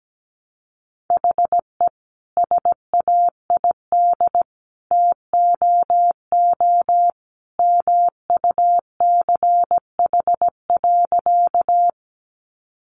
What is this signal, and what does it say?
Morse code, heard as a keyed tone, about 17 wpm, 700 Hz. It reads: HE SAID TOO MUCH.
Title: Sentences from Top 200 Words 17wpm